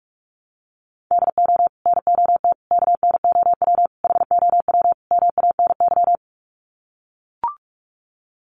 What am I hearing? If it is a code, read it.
Morse code, heard as a keyed tone, about 44 wpm, 700 Hz. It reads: DO NOT KNOW HOW MANY
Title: Sets of 5 Words Encore 45wpm